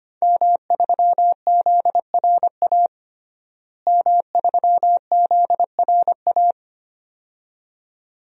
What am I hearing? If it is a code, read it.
Morse code, heard as a keyed tone, about 25 wpm, 700 Hz. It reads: M3ZRA M3ZRA
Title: DXCC-100 Call Signs - 5 Characters - Repeated 2x - 25wpm